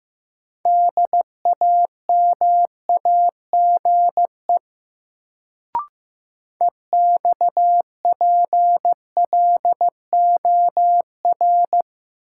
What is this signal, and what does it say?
Morse code, heard as a keyed tone, about 15 wpm, 700 Hz. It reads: DAMAGE EXPLOR
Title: Sentences from Top 1100 Words - New Word Drill Alt - 15wpm